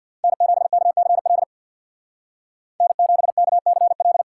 Morse code, heard as a keyed tone, about 58 wpm, 700 Hz. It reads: N7KCL N7KCL
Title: Speed Racing - USA 1x3 Call Signs 50wpm